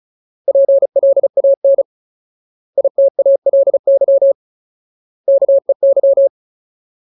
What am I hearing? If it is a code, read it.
Morse code, heard as a keyed tone, about 36 wpm, 550 Hz. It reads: PLAN ITALY KEY
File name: Ultimate Word List / 35WPM Ultimate